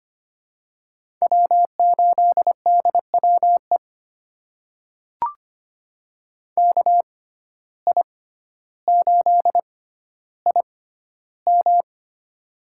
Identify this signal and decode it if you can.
Morse code, heard as a keyed tone, about 25 wpm, spaced as for 13 wpm, 700 Hz. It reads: W8DWE KI8IM
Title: USA Call Signs - 5 Characters - 6x Character Spacing - 25wpm